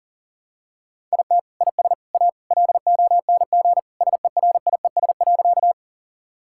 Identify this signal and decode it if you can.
Morse code, heard as a keyed tone, about 40 wpm, 700 Hz. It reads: IT IS A LONG SERIES.
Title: Sentences from Top 1000 Words - Review Part 5 - 40wpm